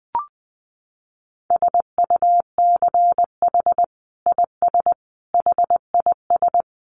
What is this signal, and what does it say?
Morse code, heard as a keyed tone, about 20 wpm, 700 Hz. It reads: SUCH IS HIS
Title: Sentences from Top 900 Words 20wpm